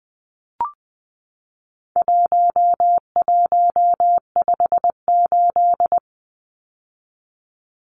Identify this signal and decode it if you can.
Morse code, heard as a keyed tone, about 20 wpm, 700 Hz. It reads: 1158
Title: Numbers - 4-Digit 20wpm